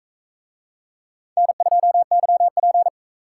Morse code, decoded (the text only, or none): N2YP